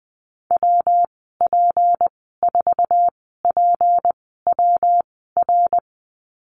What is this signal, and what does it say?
Morse code, heard as a keyed tone, about 20 wpm, spaced as for 15 wpm, 700 Hz. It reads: WP4PWR
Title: 213 Lesson 34g - Number 8 - Call Signs 20wpm